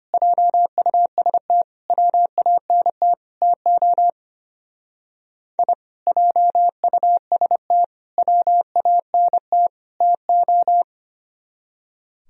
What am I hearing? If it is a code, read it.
Morse code, heard as a keyed tone, about 32 wpm, 700 Hz. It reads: JUST WANT TO I JUST WANT TO